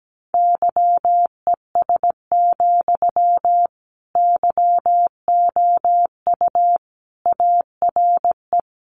Morse code, heard as a keyed tone, about 17 wpm, 700 Hz. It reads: YES, YOU ARE
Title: Sentences from Top 500 Words 17wpm